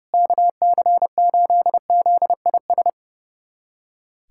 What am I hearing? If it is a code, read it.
Morse code, heard as a keyed tone, about 32 wpm, 700 Hz. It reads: KC8ZIS
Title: USA Callsign DE Callsign 30wpm